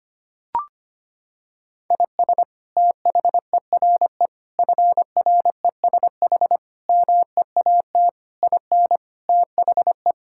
Morse code, heard as a keyed tone, about 25 wpm, 700 Hz. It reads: IS THERE FRESH MEAT IN THE
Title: Sentences from Top 1600 Words 25wpm